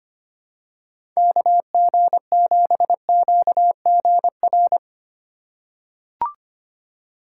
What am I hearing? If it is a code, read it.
Morse code, heard as a keyed tone, about 25 wpm, 700 Hz. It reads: KG7QGR
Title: USA Call Signs - 6 Characters - 3x Character Spacing - 25wpm